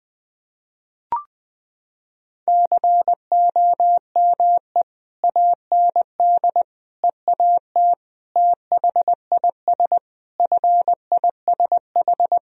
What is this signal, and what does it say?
Morse code, heard as a keyed tone, about 21 wpm, 700 Hz. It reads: COME AND EAT THIS FISH
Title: Sentences from Top 1000 Words - Review Part 1 - 20wpm